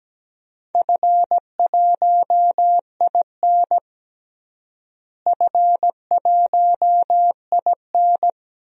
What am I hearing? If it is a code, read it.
Morse code, heard as a keyed tone, about 17 wpm, 700 Hz. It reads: F1IN F1IN
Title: DXCC-100 Call Signs - 4 Characters - Repeated 3x - 17wpm